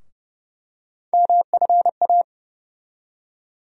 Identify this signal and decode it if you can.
Morse code, heard as a keyed tone, about 32 wpm, 700 Hz. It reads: MFA
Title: USA Call Sign Postfixes - 3 Letters 30wpm